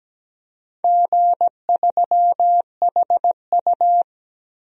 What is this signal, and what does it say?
Morse code, heard as a keyed tone, about 17 wpm, 700 Hz. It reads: G3HU
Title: DXCC-100 Call Signs - 4 Characters - 17wpm